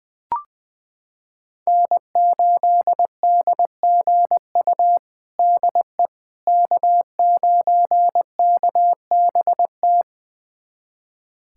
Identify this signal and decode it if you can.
Morse code, heard as a keyed tone, about 20 wpm, 700 Hz. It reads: N8DGU DE K9KBT
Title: USA Callsign DE Callsign - 5 Characters - 20wpm